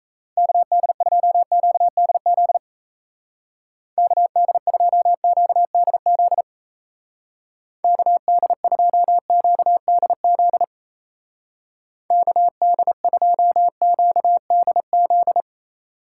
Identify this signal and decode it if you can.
Morse code, heard as a keyed tone, about 35 wpm, 700 Hz. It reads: KD2QDZ KD2QDZ KD2QDZ KD2QDZ